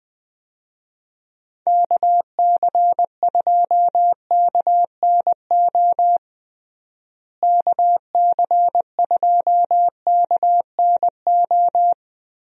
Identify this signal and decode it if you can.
Morse code, heard as a keyed tone, about 20 wpm, 700 Hz. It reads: KC2KNO KC2KNO
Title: USA Call Signs - 6 Characters - Repeated 2x - 20wpm